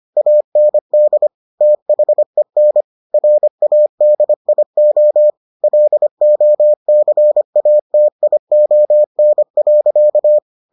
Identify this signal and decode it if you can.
Morse code, heard as a keyed tone, about 25 wpm, 600 Hz. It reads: AND THEN RADIO LOCATION.